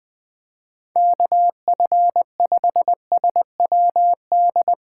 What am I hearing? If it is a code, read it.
Morse code, heard as a keyed tone, about 21 wpm, 700 Hz. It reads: KF5SWD
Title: Speed Racing - USA 2x3 Call Signs 17wpm